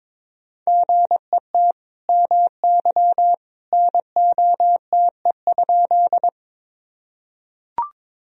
Morse code, heard as a keyed tone, about 21 wpm, 700 Hz. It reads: GET MY NOTE?
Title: Sentences from Top 1000 Words - Review Part 2 - 22wpm